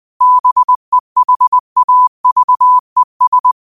Morse code, heard as a keyed tone, about 20 wpm, 1000 Hz. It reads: BEHAVES
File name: 20wpm